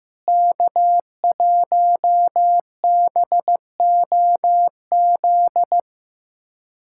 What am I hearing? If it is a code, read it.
Morse code, heard as a keyed tone, about 15 wpm, 700 Hz. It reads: K1BOZ
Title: USA Callsign DE Callsign - 5 Characters - 15wpm